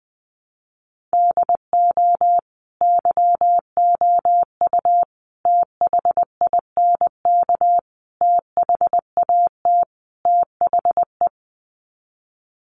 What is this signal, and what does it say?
Morse code, heard as a keyed tone, about 20 wpm, 700 Hz. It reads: DO YOU THINK THAT THE
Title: Sets of 5 Words Encore 20wpm